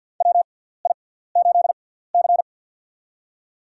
Morse code, heard as a keyed tone, about 49 wpm, spaced as for 17 wpm, 700 Hz. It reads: WI8C
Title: USA Call Signs - 4 Characters - 6x Character Spacing - 50wpm